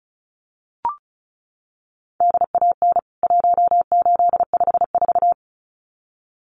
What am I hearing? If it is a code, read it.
Morse code, heard as a keyed tone, about 34 wpm, 700 Hz. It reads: DAN 1854
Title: CWT Exchange 35wpm